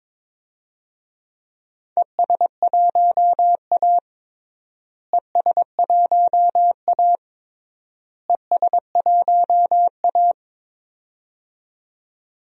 Morse code, heard as a keyed tone, about 22 wpm, 700 Hz. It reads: ES1A ES1A ES1A
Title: DXCC-100 Call Signs - 4 Characters - Repeated 3x - 22wpm